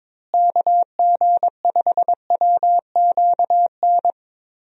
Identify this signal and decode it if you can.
Morse code, heard as a keyed tone, about 22 wpm, 700 Hz. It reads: KG5WQN
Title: USA Call Signs - 6 Characters - Repeated 3x - 22wpm